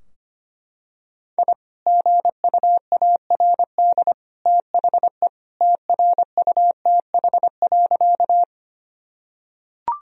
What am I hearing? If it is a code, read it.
Morse code, heard as a keyed tone, about 25 wpm, 700 Hz. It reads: I GUARD THE TRUTH.
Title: Sentences from Top 1000 Words - Review Part 4 - 25wpm